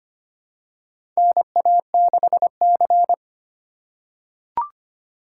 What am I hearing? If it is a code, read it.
Morse code, heard as a keyed tone, about 25 wpm, 700 Hz. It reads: NA6C
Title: USA Call Signs - 4 Characters - 2x Character Spacing - 25wpm